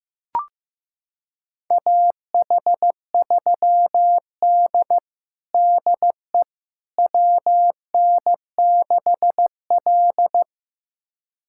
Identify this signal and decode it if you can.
Morse code, heard as a keyed tone, about 15 wpm, 700 Hz. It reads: AH3D DE WN6L
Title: USA Callsign DE Callsign - 4 Characters - 15wpm